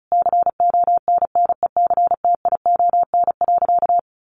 Morse code, heard as a keyed tone, about 34 wpm, 700 Hz. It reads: CONNECTION.
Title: Sentences from Top 1300 Words 35wpm